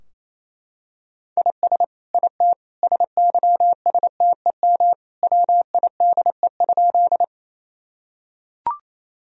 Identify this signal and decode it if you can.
Morse code, heard as a keyed tone, about 28 wpm, 700 Hz. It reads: IS IT SYSTEM WIDE?
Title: Sentences from Top 1000 Words - Review Part 3 - 28wpm